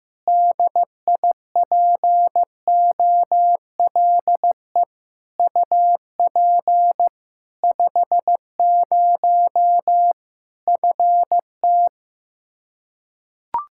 Morse code, heard as a keyed tone, about 15 wpm, 700 Hz. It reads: DIPOLE UP 50 FT
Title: Practice QSO - Exchange 2 - Info - 15wpm